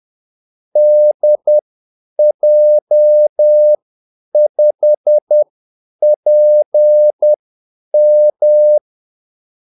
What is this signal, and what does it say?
Morse code, heard as a keyed tone, about 10 wpm, spaced as for 8 wpm, 600 Hz. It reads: DJ5PM
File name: Callsigns-10-8wpm